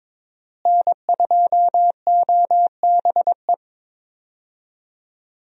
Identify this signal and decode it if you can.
Morse code, heard as a keyed tone, about 22 wpm, 700 Hz. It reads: N2OBE
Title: USA Call Signs - 5 Characters - Repeated 3x - 22wpm